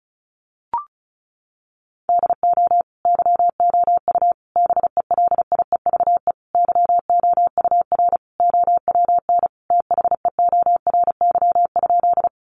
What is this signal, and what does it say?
Morse code, heard as a keyed tone, about 34 wpm, 700 Hz. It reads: DO YOU BELIEVE YOUR OWN THEORY?
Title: Sentences from Top 1000 Words 35wpm